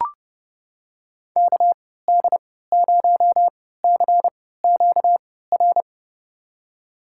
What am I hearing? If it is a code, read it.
Morse code, heard as a keyed tone, about 32 wpm, spaced as for 17 wpm, 700 Hz. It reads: KD0CQR
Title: USA Call Signs - 3x Character Spacing - 30wpm